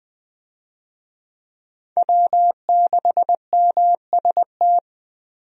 Morse code, heard as a keyed tone, about 20 wpm, 700 Hz. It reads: W6MST
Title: USA Call Signs - 5 Characters - Repeated 3x - 20wpm